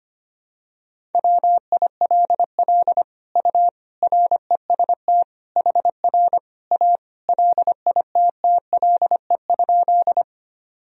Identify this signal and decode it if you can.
Morse code, heard as a keyed tone, about 25 wpm, 700 Hz. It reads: WILL U REST HR A LITTLE?
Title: Sentences from Top 500 Words and Abbreviations 25wpm